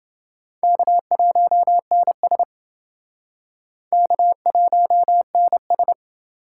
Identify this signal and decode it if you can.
Morse code, heard as a keyed tone, about 29 wpm, 700 Hz. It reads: K1NS K1NS